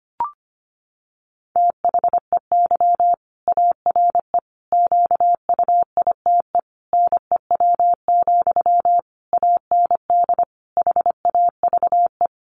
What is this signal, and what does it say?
Morse code, heard as a keyed tone, about 25 wpm, 700 Hz. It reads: THEY ARE QUITE NEW, AND HAVE
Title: Sentences from Top 700 Words 25wpm